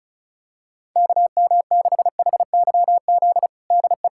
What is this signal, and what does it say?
Morse code, heard as a keyed tone, about 36 wpm, 700 Hz. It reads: KM6HYZ DE
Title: USA Callsign DE Callsign - 6 Characters - 35wpm